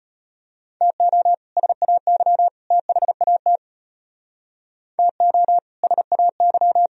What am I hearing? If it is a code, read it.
Morse code, heard as a keyed tone, about 32 wpm, 700 Hz. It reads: TO SAY THAT TO SAY